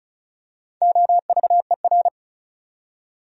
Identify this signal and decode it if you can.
Morse code, heard as a keyed tone, about 36 wpm, 700 Hz. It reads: OVER
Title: Top 100 Words 35wpm